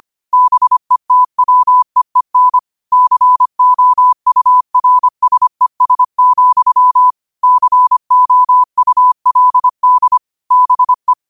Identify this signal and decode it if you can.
Morse code, heard as a keyed tone, about 25 wpm, 1000 Hz. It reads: BETWEEN COURSES, COULD BE
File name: qotdfiles / 25wpm